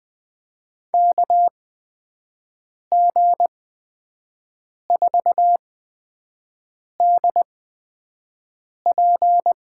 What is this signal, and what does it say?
Morse code, heard as a keyed tone, about 20 wpm, spaced as for 5 wpm, 700 Hz. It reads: KG4DP
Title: USA Call Signs - 6 Characters - 8x Character Spacing - 20wpm